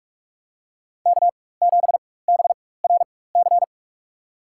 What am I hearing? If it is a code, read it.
Morse code, heard as a keyed tone, about 44 wpm, spaced as for 21 wpm, 690 Hz. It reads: K7BRC
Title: USA Call Signs - 5 Characters - 4x Character Spacing - 45wpm